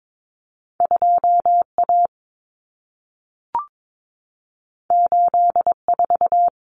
Morse code, heard as a keyed tone, about 22 wpm, 700 Hz. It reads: 2A 84
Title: Common Cut Numbers - 2-Digit 22wpm